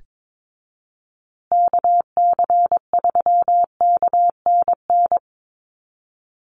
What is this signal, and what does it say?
Morse code, heard as a keyed tone, about 22 wpm, 700 Hz. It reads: KC3KNN
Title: USA Call Signs Repeated 3x 22wpm